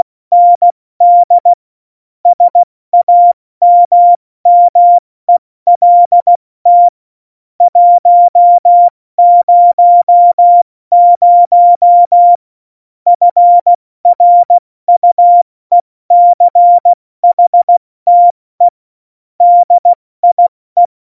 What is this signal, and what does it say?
Morse code, heard as a keyed tone, about 16 wpm, 700 Hz. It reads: ND SAMMELT 100 FRUECHTE DIE